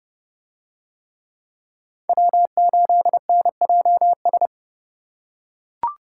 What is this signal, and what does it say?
Morse code, heard as a keyed tone, about 29 wpm, 700 Hz. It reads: W8NJS